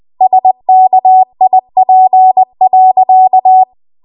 Morse code, heard as a keyed tone, about 20 wpm, 750 Hz. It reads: SKIP.
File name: Morse / 20WPM